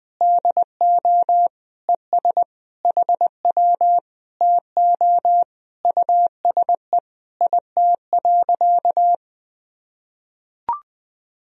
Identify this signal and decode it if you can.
Morse code, heard as a keyed tone, about 21 wpm, 700 Hz. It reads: DO ES HW TO USE IT.
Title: Sentences from Top 500 Words and Abbreviations 20wpm